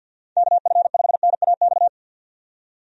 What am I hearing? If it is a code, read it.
Morse code, heard as a keyed tone, about 48 wpm, 700 Hz. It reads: KF5NAX
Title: USA Call Signs - 6 Characters - 50wpm